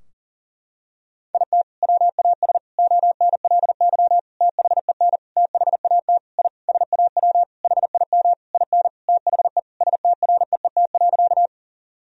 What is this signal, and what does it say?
Morse code, heard as a keyed tone, about 40 wpm, 700 Hz. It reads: IT WAS ONLY THEN THAT I SAW HIM IN THE STREET.
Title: Sentences from Top 1000 Words - Review Part 3 - 40wpm